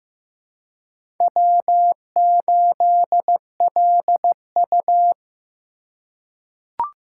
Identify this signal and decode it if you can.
Morse code, heard as a keyed tone, about 15 wpm, 700 Hz. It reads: W8LU
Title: USA Call Signs - 4 Characters - 2x Character Spacing - 15wpm